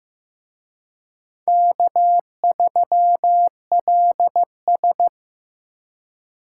Morse code, heard as a keyed tone, about 18 wpm, 700 Hz. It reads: K3LS
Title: Speed Racing - USA Call Signs 15wpm